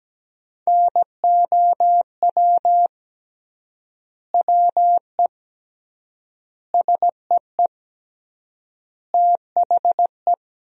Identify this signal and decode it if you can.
Morse code, heard as a keyed tone, about 17 wpm, 700 Hz. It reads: NOW WE SEE THE
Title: 3x Word-Spacing - Sentences from Top 200 Words - 17wpm